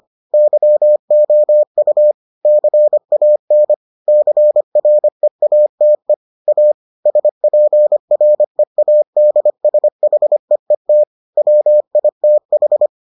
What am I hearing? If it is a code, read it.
Morse code, heard as a keyed tone, about 25 wpm, 600 Hz. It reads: YOU CAN CREATE A SPREADSHEET WITH